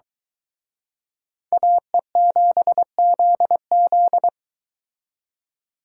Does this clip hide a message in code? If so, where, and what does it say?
Yes: Morse code from 1.5 to 4.3 s, heard as a keyed tone, about 23 wpm, 700 Hz. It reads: AE7ZZ